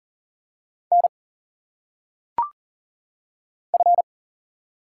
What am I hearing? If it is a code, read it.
Morse code, heard as a keyed tone, about 40 wpm, 700 Hz. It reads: N F
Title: Single Letters - Phonetic Pronunciation 40wpm